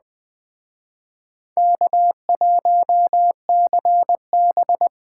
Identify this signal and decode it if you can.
Morse code, heard as a keyed tone, about 20 wpm, 700 Hz. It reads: K1CB
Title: Speed Racing - USA 1x2 Call Signs 17wpm